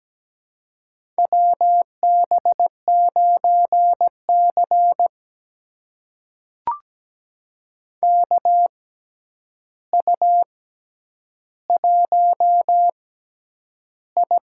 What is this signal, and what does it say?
Morse code, heard as a keyed tone, about 17 wpm, spaced as for 9 wpm, 700 Hz. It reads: WB9C KU1I
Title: USA Call Signs - 6x Character Spacing - 17wpm